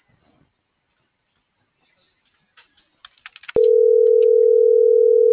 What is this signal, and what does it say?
Ringback tone (North America), 440 + 480 Hz.